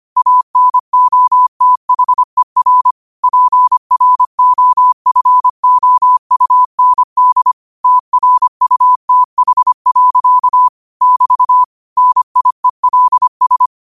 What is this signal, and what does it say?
Morse code, heard as a keyed tone, about 25 wpm, 1000 Hz. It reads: ANOTHER PROFOUND TRUTH. <BT> NIELS
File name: qotdfiles / 25wpm